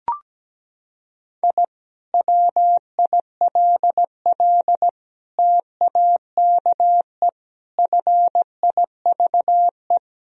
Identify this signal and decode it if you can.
Morse code, heard as a keyed tone, about 17 wpm, 700 Hz. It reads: I WILL TAKE FIVE
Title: Sentences from Top 1000 Words - Review Part 2 - 17wpm